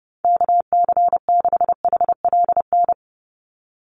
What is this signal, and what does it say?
Morse code, heard as a keyed tone, about 28 wpm, 700 Hz. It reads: KC6HLN
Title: USA Call Signs 30wpm